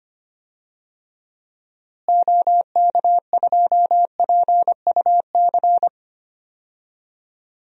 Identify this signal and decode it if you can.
Morse code, heard as a keyed tone, about 25 wpm, 700 Hz. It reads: OK2PUC
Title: DXCC-100 Call Signs - 6 Characters - Repeated 3x - 25wpm